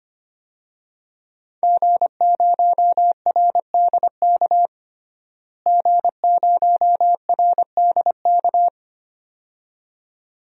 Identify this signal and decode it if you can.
Morse code, heard as a keyed tone, about 25 wpm, 700 Hz. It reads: G0RDK G0RDK
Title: DXCC-100 Call Signs - 5 Characters - Repeated 2x - 25wpm